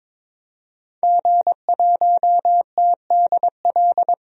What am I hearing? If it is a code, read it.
Morse code, heard as a keyed tone, about 22 wpm, 700 Hz. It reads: G1TDL
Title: DXCC-100 Call Signs - 5 Characters - 22wpm